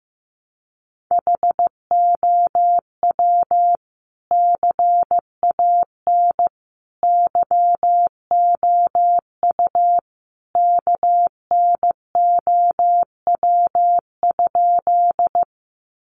Morse code, heard as a keyed tone, about 15 wpm, 700 Hz. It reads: HOW CAN YOU KNOW?
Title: Sentences from Top 100 Words 15wpm